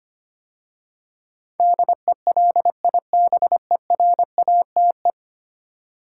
Morse code, heard as a keyed tone, about 25 wpm, 690 Hz. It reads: DELIBERATE
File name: 10 Letter Words - First 3 Letters Given - 25wpm